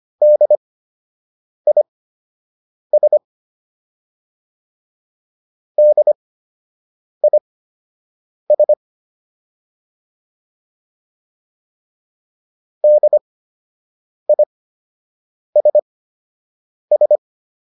Morse code, heard as a keyed tone, about 25 wpm, spaced as for 7 wpm, 600 Hz. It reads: DIS DIS DISS